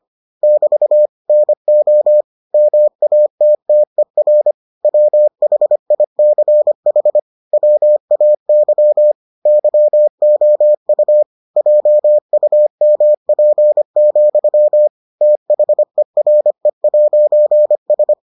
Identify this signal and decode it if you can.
Morse code, heard as a keyed tone, about 25 wpm, 600 Hz. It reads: <BT> NO MATTER WHICH WAY YOU JUMP, THERE'S